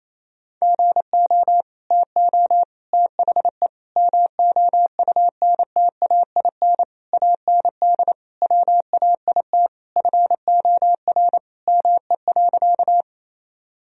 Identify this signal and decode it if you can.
Morse code, heard as a keyed tone, about 28 wpm, 700 Hz. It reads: GO TO THE MOUNTAIN AND WAIT FOR ME.
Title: Sentences from Top 1000 Words - Review Part 1 - 28wpm